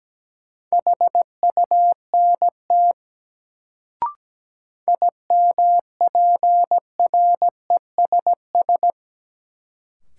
Morse code, heard as a keyed tone, about 17 wpm, 700 Hz. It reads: HUNT IMPRESS
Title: Sentences from Top 1600 Words - New Word Drill - 17wpm